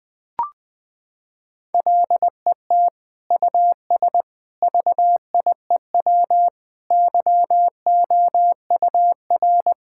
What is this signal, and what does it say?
Morse code, heard as a keyed tone, about 20 wpm, 700 Hz. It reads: LET US VIEW YOUR
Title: Sentences from Top 1000 Words - Review Part 6 - 20wpm